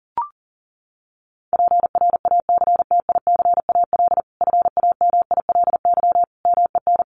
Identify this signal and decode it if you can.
Morse code, heard as a keyed tone, about 40 wpm, 700 Hz. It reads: PRACTICAL FAMILY MEN
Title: Sentences from Top 1400 Words 40wpm